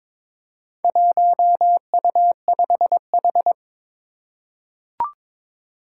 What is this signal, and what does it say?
Morse code, heard as a keyed tone, about 22 wpm, 700 Hz. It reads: 1U5H
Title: Four Challenging Letter-Number Combinations 22wpm